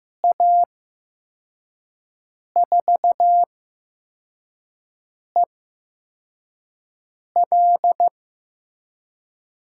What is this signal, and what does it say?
Morse code, heard as a keyed tone, about 15 wpm, spaced as for 4 wpm, 700 Hz. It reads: A4EL
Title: USA Call Signs - 5 Characters - 8x Character Spacing - 15wpm